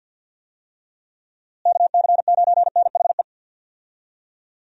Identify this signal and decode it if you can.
Morse code, heard as a keyed tone, about 46 wpm, 700 Hz. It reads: KC9NHE
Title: USA Call Signs - 6 Characters - Repeated 3x - 50wpm